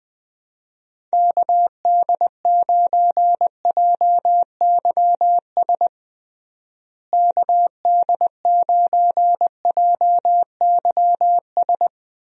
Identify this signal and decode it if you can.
Morse code, heard as a keyed tone, about 20 wpm, 700 Hz. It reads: KD9JYS KD9JYS
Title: USA Call Signs - 6 Characters - Repeated 3x - 20wpm